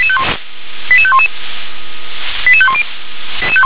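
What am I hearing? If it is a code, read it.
Five-tone selective call (ZVEI), heard as quick runs of five tones, tones of about 70 ms. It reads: 88511 88510
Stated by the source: ZVEI five-tone call received by SDR.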